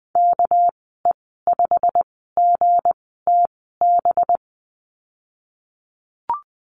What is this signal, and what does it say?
Morse code, heard as a keyed tone, about 20 wpm, spaced as for 14 wpm, 700 Hz. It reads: KE5GTB
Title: 231 Lesson 37g - Letter x - Call Signs 20wpm